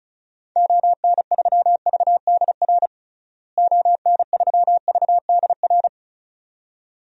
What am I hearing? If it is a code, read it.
Morse code, heard as a keyed tone, about 34 wpm, 700 Hz. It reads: ON3VDR ON3VDR
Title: DXCC-100 Call Signs - 6 Characters - Repeated 2x - 35wpm